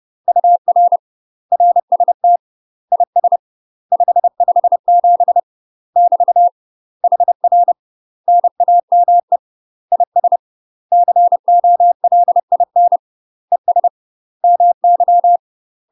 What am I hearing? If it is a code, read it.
Morse code, heard as a keyed tone, about 30 wpm, 700 Hz. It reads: UR RST IS 557 <BT> HR NAME IS COLIN ES MY
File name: UK MASTER QSOs TRAINING/30WPM